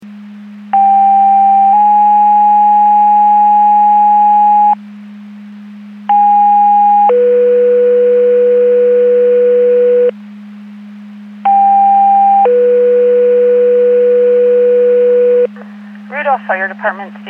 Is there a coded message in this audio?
Yes: two-tone page from 0.7 to 15.5 s, heard as a one-second tone, then a three-second tone, about 790 then 830 Hz, about 820 then 490 Hz, about 800 then 490 Hz, with speech.